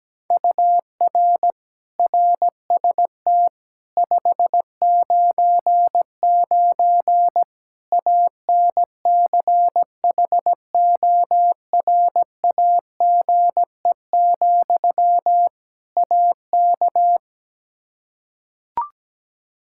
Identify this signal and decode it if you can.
Morse code, heard as a keyed tone, about 17 wpm, 700 Hz. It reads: UR RST 599 ANCHORAGE, AK
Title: K1USN SST QSO Practice 17wpm